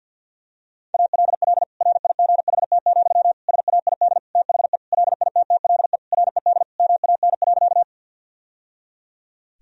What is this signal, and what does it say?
Morse code, heard as a keyed tone, about 50 wpm, 700 Hz. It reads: ALL RIGHT, SAID THE LITTLE RED MAN.
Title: Sentences from Top 1000 Words - Review Part 2 - 50wpm